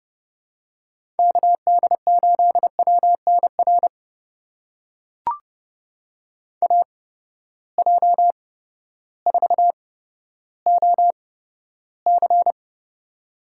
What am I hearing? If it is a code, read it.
Morse code, heard as a keyed tone, about 30 wpm, spaced as for 14 wpm, 700 Hz. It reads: KD8WNR AJ4OC
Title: USA Call Signs - 8x Character Spacing - 30wpm